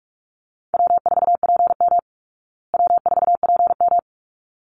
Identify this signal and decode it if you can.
Morse code, heard as a keyed tone, about 44 wpm, 700 Hz. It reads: W4PM W4PM
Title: Call Signs Repeated 3x - 45wpm